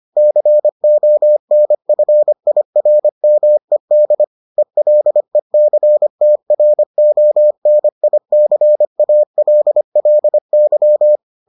Morse code, heard as a keyed tone, about 25 wpm, 600 Hz. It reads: CONFIRMED ELECTRONICALLY